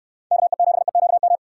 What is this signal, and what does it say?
Morse code, heard as a keyed tone, about 67 wpm, 700 Hz. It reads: KE7EQM